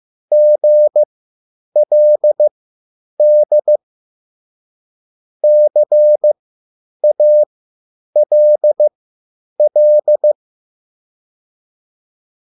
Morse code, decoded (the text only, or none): GLD CALL